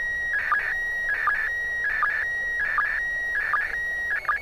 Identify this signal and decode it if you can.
SSTV (slow-scan television), PD 180.
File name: ISS 2015-04-11 145 800MHz SSTV